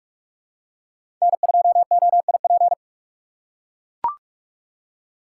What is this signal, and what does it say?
Morse code, heard as a keyed tone, about 43 wpm, 700 Hz. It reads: N2OIP
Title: USA Call Signs - 5 Characters - 4x Character Spacing - 45wpm